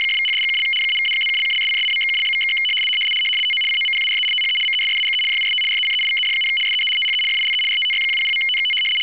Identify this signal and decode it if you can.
RTTY, about 100 baud. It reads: !=#$&() -=-+()()':',./()?=// TO COMPARE THE MESSAGE JUST RECEIVED, SEE THE FILE RTTY100.TXT
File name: w ry850 / rtty100